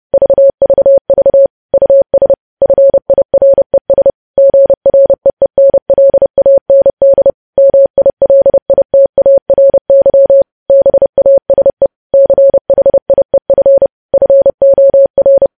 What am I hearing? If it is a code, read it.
Morse code, heard as a keyed tone, about 30 wpm, 550 Hz. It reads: VVV US FIRES GREENLAND MILITARY BASE CHIEF FOR
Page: news headlines in morse code at 30 wpm